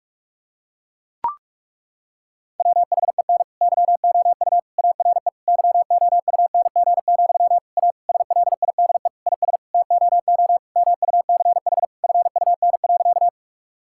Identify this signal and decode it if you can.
Morse code, heard as a keyed tone, about 41 wpm, 700 Hz. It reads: WHEN YOU ARE YOUNG, A SLIDE IS TOO MUCH FUN.
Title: Sentences from Top 1600 Words 45wpm